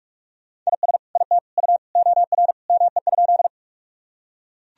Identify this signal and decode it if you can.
Morse code, heard as a keyed tone, about 41 wpm, 700 Hz. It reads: IS IT U OR ME?
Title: Sentences from Top 200 Words and Abbreviations 45wpm